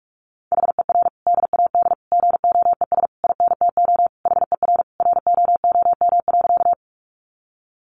Morse code, heard as a keyed tone, about 44 wpm, 700 Hz. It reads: HER DAD GOES INTO HER ROOM.
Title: Additional Easy Phrases 45wpm